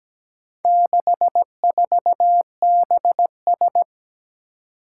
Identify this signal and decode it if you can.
Morse code, heard as a keyed tone, about 17 wpm, 700 Hz. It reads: 64BS
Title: Four Challenging Letter-Number Combinations 17wpm